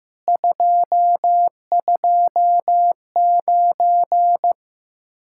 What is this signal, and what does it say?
Morse code, heard as a keyed tone, about 15 wpm, 700 Hz. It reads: 229
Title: Speed-Racing Common QSO Phrases Encore - 15wpm